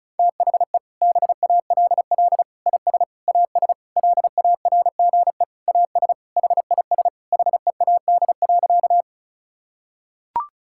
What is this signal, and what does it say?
Morse code, heard as a keyed tone, about 36 wpm, 700 Hz. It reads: THE BALL IS AS LARGE AS HIS HEAD.
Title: Sentences from Top 1000 Words - Review Part 6 - 35wpm